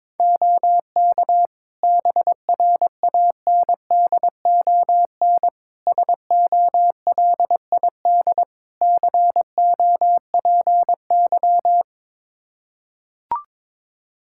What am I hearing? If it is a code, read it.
Morse code, heard as a keyed tone, about 21 wpm, 700 Hz. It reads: OK BRANDON SOLID COPY
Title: K1USN SST QSO Practice 22wpm